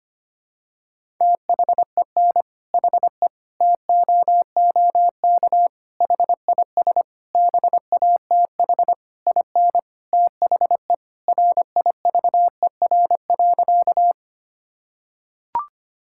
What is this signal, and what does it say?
Morse code, heard as a keyed tone, about 25 wpm, 700 Hz. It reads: THEN HE TOOK HIS BATH IN THE RIVER.
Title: Sentences from Top 1600 Words 25wpm